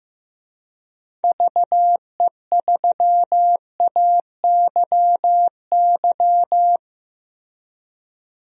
Morse code, heard as a keyed tone, about 15 wpm, 700 Hz. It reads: VE3AYY
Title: DXCC-100 Call Signs - 6 Characters - Repeated 3x - 15wpm